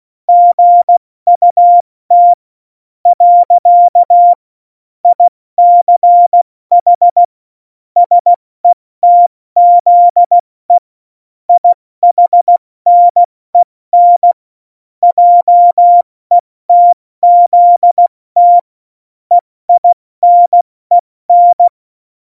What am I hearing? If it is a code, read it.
Morse code, heard as a keyed tone, about 16 wpm, spaced as for 14 wpm, 700 Hz. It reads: GUT . ICH SETZE IHNEN JETZT EINEN
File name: Morsekurs / Witze / Witz 13 March 2018, 70 F80